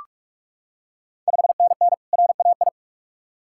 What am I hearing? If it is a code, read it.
Morse code, heard as a keyed tone, about 41 wpm, 700 Hz. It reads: 5NN RAI
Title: Ontario QSO Party Exchange 45wpm